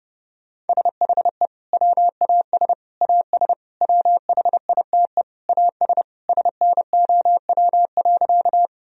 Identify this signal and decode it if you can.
Morse code, heard as a keyed tone, about 32 wpm, 700 Hz. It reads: SHE WAS AS WHITE AS SNOW.
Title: Sentences from Top 1000 Words - Review Part 3 - 30wpm